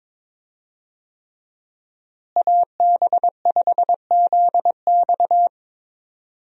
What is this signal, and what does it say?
Morse code, heard as a keyed tone, about 22 wpm, 700 Hz. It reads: AB5ZX
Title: USA Call Signs - 5 Characters - Repeated 3x - 22wpm